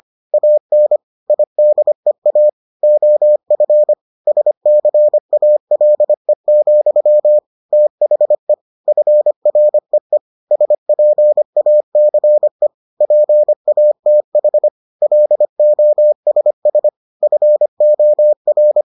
Morse code, heard as a keyed tone, about 25 wpm, 600 Hz. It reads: AN IDEA OF SCALE, THE FREE SPACE PATH LOSS FOR